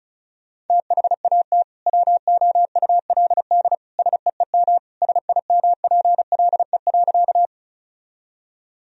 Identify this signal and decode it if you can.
Morse code, heard as a keyed tone, about 36 wpm, 700 Hz. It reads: THAT WOULD SEEM SIMPLE.
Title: Sentences from Top 1000 Words - Review Part 1 - 35wpm